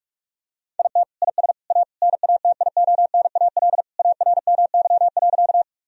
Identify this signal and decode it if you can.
Morse code, heard as a keyed tone, about 46 wpm, 700 Hz. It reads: IT IS A NATIONAL ARMY.
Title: Sentences from Top 1000 Words - Review Part 4 - 45wpm